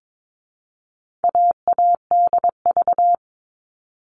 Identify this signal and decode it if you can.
Morse code, heard as a keyed tone, about 22 wpm, 700 Hz. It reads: AADV
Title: All Cut Numbers - 4-Digit 22wpm